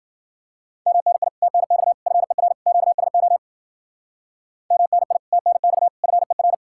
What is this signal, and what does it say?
Morse code, heard as a keyed tone, about 48 wpm, 700 Hz. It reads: MNI TNX FER QSO MNI TNX FER
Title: Speed-Racing Common QSO Phrases - 40wpm